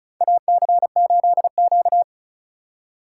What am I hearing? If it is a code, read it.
Morse code, heard as a keyed tone, about 36 wpm, 700 Hz. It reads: AC8Q
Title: USA Call Signs - 4 Characters - 2x Character Spacing - 35wpm